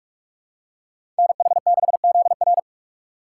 Morse code, heard as a keyed tone, about 46 wpm, 700 Hz. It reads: NH6ZR
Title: USA Call Signs - 5 Characters - 3x Character Spacing - 45wpm